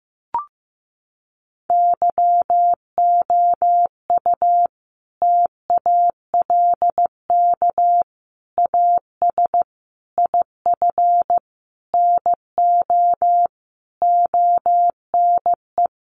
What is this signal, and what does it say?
Morse code, heard as a keyed tone, about 15 wpm, 700 Hz. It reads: YOU TALK AS IF NO ONE
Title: Sentences from Top 1100 Words 15wpm